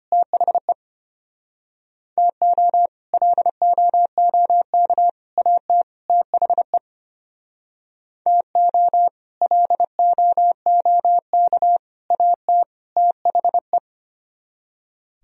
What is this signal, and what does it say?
Morse code, heard as a keyed tone, about 31 wpm, 700 Hz. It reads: THE TO LOOK AT THE TO LOOK AT THE
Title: Speed-Racing - Sets of 4 Words 25wpm